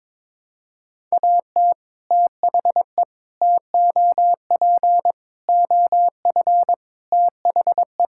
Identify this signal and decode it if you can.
Morse code, heard as a keyed tone, about 22 wpm, 700 Hz. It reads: AT THE TOP OF THE
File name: practice / 22wpm / Sets of 7 Words - 22wpm